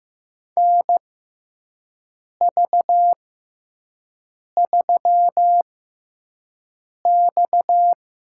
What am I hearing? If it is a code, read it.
Morse code, heard as a keyed tone, about 15 wpm, spaced as for 5 wpm, 700 Hz. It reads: NV3X
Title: USA Call Signs - 4 Characters - 6x Character Spacing - 15wpm